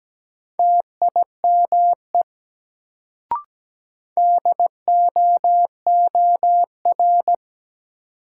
Morse code, heard as a keyed tone, about 17 wpm, 700 Hz. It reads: TIME DOOR
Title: Sets of 2 Words - New Word Drill - 17wpm